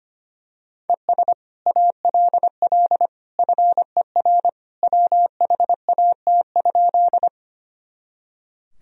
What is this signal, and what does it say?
Morse code, heard as a keyed tone, about 25 wpm, 700 Hz. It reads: ES ALL FER WHAT?
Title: Sentences from Top 100 Words and Abbreviations 25wpm